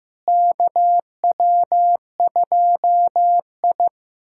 Morse code, heard as a keyed tone, about 15 wpm, 700 Hz. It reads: KW2I
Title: USA Callsign DE Callsign - 4 Characters - 15wpm